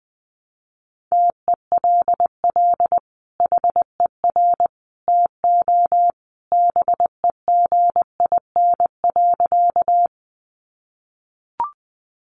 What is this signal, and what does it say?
Morse code, heard as a keyed tone, about 20 wpm, 700 Hz. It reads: TELL HER TO BEGIN.
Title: Sentences from Top 300 Words 20wpm